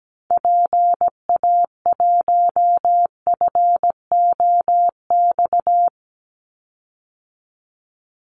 Morse code, heard as a keyed tone, about 17 wpm, 700 Hz. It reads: PA1FOX
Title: Call Signs - 17wpm